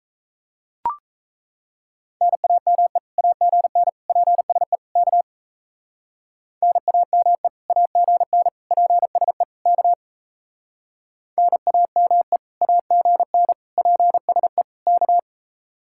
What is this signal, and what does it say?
Morse code, heard as a keyed tone, about 37 wpm, 700 Hz. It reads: NAME AGN PSE K NAME AGN PSE K NAME AGN PSE K